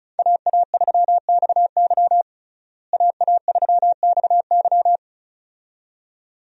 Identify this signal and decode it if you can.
Morse code, heard as a keyed tone, about 34 wpm, 700 Hz. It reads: AA3XY AA3XY
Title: USA Call Signs - 5 Characters - Repeated 2x - 35wpm